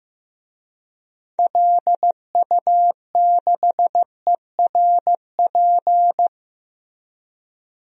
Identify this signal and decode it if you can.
Morse code, heard as a keyed tone, about 15 wpm, 700 Hz. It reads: LU6ERP
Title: DXCC-100 Call Signs - 6 Characters - Repeated 2x - 15wpm